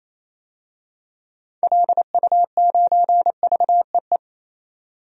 Morse code, heard as a keyed tone, about 27 wpm, 700 Hz. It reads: LU9VEE
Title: DXCC-100 Call Signs - 6 Characters - 28wpm